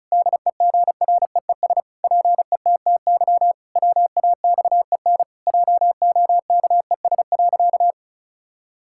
Morse code, heard as a keyed tone, about 36 wpm, 700 Hz. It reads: DEGREES PETTY WAXEN JOKES.
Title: Pangram Challenge v2 - 35wpm